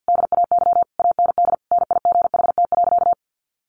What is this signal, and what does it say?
Morse code, heard as a keyed tone, about 50 wpm, 700 Hz. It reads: DAY AND NIGHT.
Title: Sentences from Top 1500 Words 50wpm